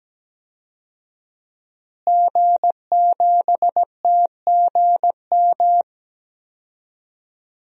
Morse code, heard as a keyed tone, about 17 wpm, 700 Hz. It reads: G7TGM
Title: DXCC-100 Call Signs - 5 Characters - Repeated 2x - 17wpm